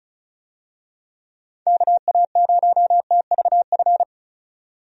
Morse code, heard as a keyed tone, about 36 wpm, 700 Hz. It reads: KA0TVF
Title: USA Call Signs - 6 Characters - 35wpm